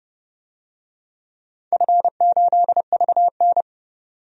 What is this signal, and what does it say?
Morse code, heard as a keyed tone, about 30 wpm, 700 Hz. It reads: F8VN
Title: DXCC-100 Call Signs - 4 Characters - 30wpm